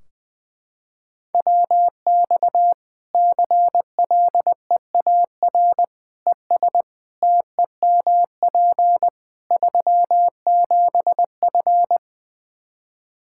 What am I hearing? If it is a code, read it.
Morse code, heard as a keyed tone, about 20 wpm, 700 Hz. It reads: WX CLEAR ES TEMP 37F
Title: K1USN SST QSO Practice 20wpm